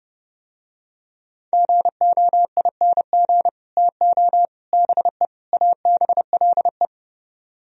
Morse code, heard as a keyed tone, about 31 wpm, 700 Hz. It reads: GOING TO BE ABLE